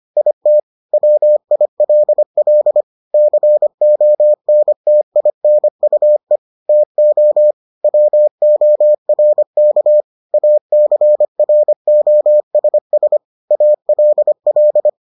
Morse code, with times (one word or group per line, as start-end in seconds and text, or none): 0.2-0.6 IT
0.9-2.8 WILL
3.1-6.4 CONTINUE
6.7-7.5 TO
7.8-10.0 WORK
10.3-13.2 ACROSS
13.5-14.9 ALL